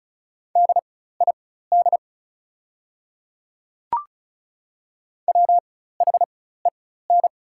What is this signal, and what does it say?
Morse code, heard as a keyed tone, about 36 wpm, spaced as for 16 wpm, 700 Hz. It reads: DID WHEN
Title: 4x Character Spacing - Top 500 Words - 35wpm